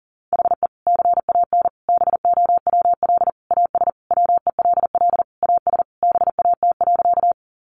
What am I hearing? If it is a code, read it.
Morse code, heard as a keyed tone, about 37 wpm, 700 Hz. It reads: HE CAN BOWL AS WELL AS BAT.
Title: Sentences from Top 1500 Words 40wpm